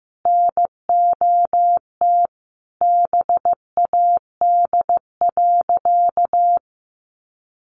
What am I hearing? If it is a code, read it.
Morse code, heard as a keyed tone, about 15 wpm, 700 Hz. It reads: NOT BAD.
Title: Sentences from Top 900 Words 15wpm